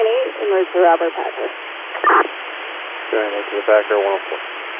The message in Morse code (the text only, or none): none